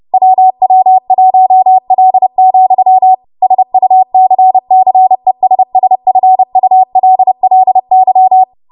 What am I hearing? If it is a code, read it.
Morse code, heard as a keyed tone, about 30 wpm, 750 Hz. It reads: WW1L, SUCCESSFULLY
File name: Morse / 30WPM